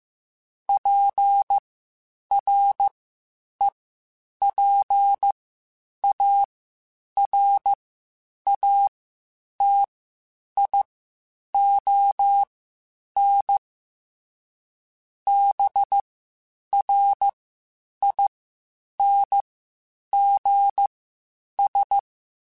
Morse code, heard as a keyed tone, about 15 wpm, spaced as for 8 wpm, 800 Hz. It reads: PREPARATION BRINGS
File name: Quotes Spaced / Quotes Spaced-mp3s / Persperation sp @13